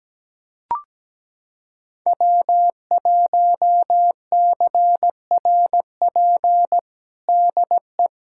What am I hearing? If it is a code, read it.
Morse code, heard as a keyed tone, about 17 wpm, 700 Hz. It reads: W1CRP DE
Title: USA Callsign DE Callsign - 5 Characters - 17wpm